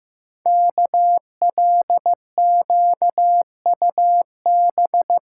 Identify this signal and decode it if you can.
Morse code, heard as a keyed tone, about 15 wpm, 700 Hz. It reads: KLQUB
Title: Code Groups - 5 Letter:Numbers - 15wpm